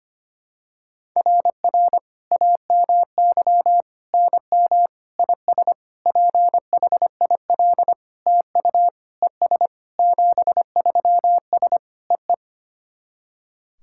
Morse code, heard as a keyed tone, about 25 wpm, 700 Hz. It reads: RR AMY NM IS PHIL TU ES 73S EE
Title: SOTA QSO Practice 25wpm